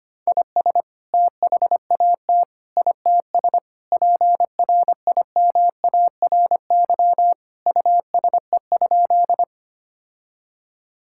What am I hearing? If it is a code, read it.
Morse code, heard as a keyed tone, about 25 wpm, 700 Hz. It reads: IS THAT ITS PRIMARY USE?
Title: Sentences from Top 1000 Words - Review Part 5 - 25wpm